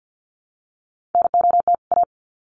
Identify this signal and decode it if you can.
Morse code, heard as a keyed tone, about 50 wpm, 700 Hz. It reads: NOT A
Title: Sets of 2 Words 50wpm